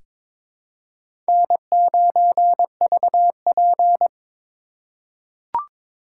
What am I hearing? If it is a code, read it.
Morse code, heard as a keyed tone, about 21 wpm, 700 Hz. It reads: N9VP
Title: USA Call Signs - 4 Characters - 3x Character Spacing - 22wpm